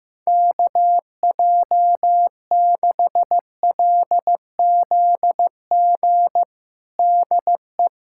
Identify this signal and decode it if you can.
Morse code, heard as a keyed tone, about 16 wpm, 700 Hz. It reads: KJ6LZG DE
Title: USA Callsign DE Callsign - 6 Characters - 15wpm